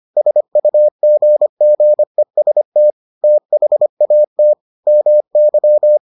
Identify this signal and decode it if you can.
Morse code, heard as a keyed tone, about 25 wpm, 600 Hz. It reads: SUGGEST THAT MY